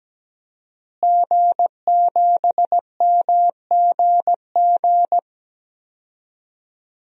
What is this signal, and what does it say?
Morse code, heard as a keyed tone, about 17 wpm, 700 Hz. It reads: G7MGG
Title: DXCC-100 Call Signs - 5 Characters - Repeated 2x - 17wpm